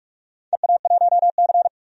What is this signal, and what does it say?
Morse code, heard as a keyed tone, about 41 wpm, 700 Hz. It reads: EA1C
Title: DXCC-100 Call Signs - 4 Characters - 45wpm